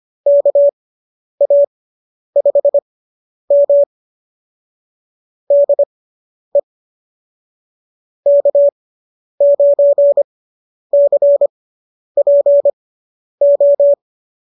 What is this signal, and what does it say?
Morse code, heard as a keyed tone, about 25 wpm, spaced as for 10 wpm, 590 Hz. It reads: KA5M DE K9CPO